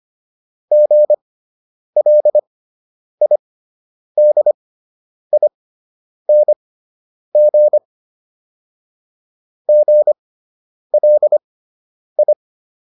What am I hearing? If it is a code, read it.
Morse code, heard as a keyed tone, about 25 wpm, spaced as for 9 wpm, 600 Hz. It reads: GLIDING GLI